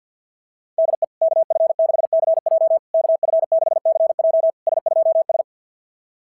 Morse code, heard as a keyed tone, about 50 wpm, 650 Hz. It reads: DE KF6CJ KF6CJ S2S
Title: SOTA QSO Practice 50wpm